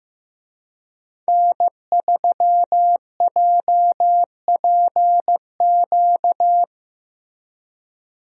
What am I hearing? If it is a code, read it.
Morse code, heard as a keyed tone, about 15 wpm, 700 Hz. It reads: N3JPQ
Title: USA Call Signs - 5 Characters - Repeated 2x - 15wpm